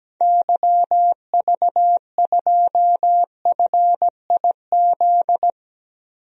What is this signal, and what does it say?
Morse code, heard as a keyed tone, about 17 wpm, 700 Hz. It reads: YV2FIZ
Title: DXCC-100 Call Signs - 6 Characters - Repeated 2x - 17wpm